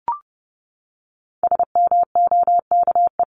Morse code, heard as a keyed tone, about 30 wpm, 700 Hz. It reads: SMOKE
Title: Top 1501-1750 Words 30wpm